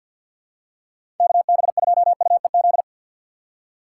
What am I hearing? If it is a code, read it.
Morse code, heard as a keyed tone, about 51 wpm, 700 Hz. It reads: KB2UEZ
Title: USA Call Signs - 6 Characters - 4x Character Spacing - 50wpm